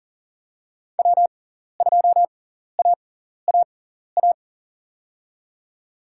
Morse code, heard as a keyed tone, about 42 wpm, spaced as for 14 wpm, 700 Hz. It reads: W2AAA
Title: USA Call Signs - 5 Characters - 6x Character Spacing - 40wpm